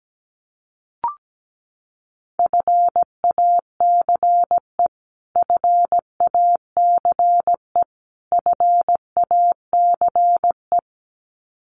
Morse code, heard as a keyed tone, about 17 wpm, 700 Hz. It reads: FACE FACE FACE
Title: Top 500 Words Repeated 17wpm